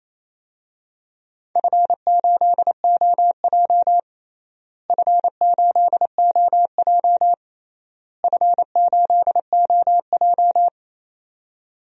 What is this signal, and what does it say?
Morse code, heard as a keyed tone, about 28 wpm, 700 Hz. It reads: F8OJ F8OJ F8OJ
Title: DXCC-100 Call Signs - 4 Characters - Repeated 3x - 28wpm